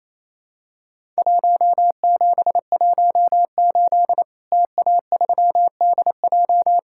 Morse code, heard as a keyed tone, about 28 wpm, 700 Hz. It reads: 1718 TA3DJ
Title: Worked All Europe DX Contest - QTC Exchange - 28wpm